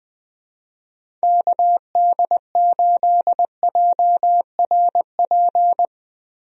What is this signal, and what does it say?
Morse code, heard as a keyed tone, about 20 wpm, 700 Hz. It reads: KD8JRP
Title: USA Call Signs - 6 Characters - Repeated 3x - 20wpm